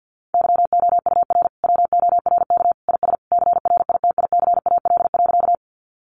Morse code, heard as a keyed tone, about 50 wpm, 700 Hz. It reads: YOUR WORK IS CRITICAL.
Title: Sentences from Top 1100 Words 50wpm